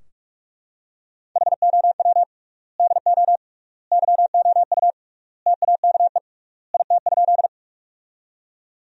Morse code, heard as a keyed tone, about 44 wpm, 700 Hz. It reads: HOW DO YOU TAKE IT?
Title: 3x Word-Spacing - Sentences from Top 200 Words - 45wpm